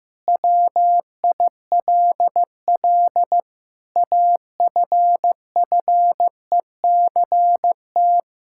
Morse code, heard as a keyed tone, about 15 wpm, 700 Hz. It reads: WILL AFFECT
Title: Sentences from Top 1000 Words - Review Part 6 - 15wpm